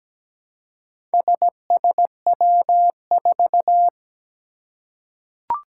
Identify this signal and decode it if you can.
Morse code, heard as a keyed tone, about 17 wpm, 700 Hz. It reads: SSW4
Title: Four Challenging Letter-Number Combinations 17wpm